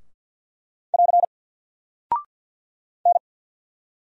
Morse code, heard as a keyed tone, about 48 wpm, 700 Hz. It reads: <AR> N
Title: Single Character (Letter:Number:Punctuation:Prosign) Alt 50wpm